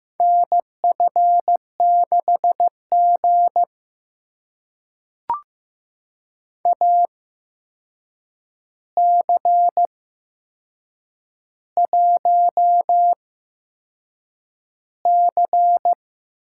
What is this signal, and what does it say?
Morse code, heard as a keyed tone, about 15 wpm, spaced as for 6 wpm, 700 Hz. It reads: NF6G AC1C
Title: USA Call Signs - 4 Characters - 8x Character Spacing - 15wpm